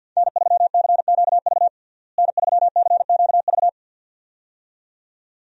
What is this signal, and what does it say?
Morse code, heard as a keyed tone, about 52 wpm, 700 Hz. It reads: N3CQV N3CQV
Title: USA Call Signs - 5 Characters - Repeated 2x - 50wpm